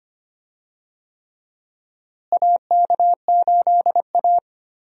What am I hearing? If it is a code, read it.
Morse code, heard as a keyed tone, about 25 wpm, 700 Hz. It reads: AK8A
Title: USA Call Signs - 4 Characters - Repeated 3x - 25wpm